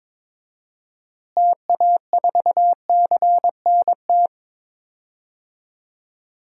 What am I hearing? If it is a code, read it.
Morse code, heard as a keyed tone, about 22 wpm, 700 Hz. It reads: TA4CNT
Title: DXCC-100 Call Signs - 6 Characters - Repeated 2x - 22wpm